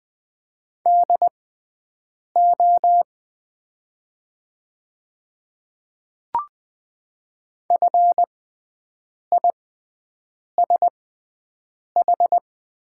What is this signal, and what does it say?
Morse code, heard as a keyed tone, about 20 wpm, spaced as for 7 wpm, 700 Hz. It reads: DO FISH
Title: 6x Character Spacing - Top 500 Words - 20wpm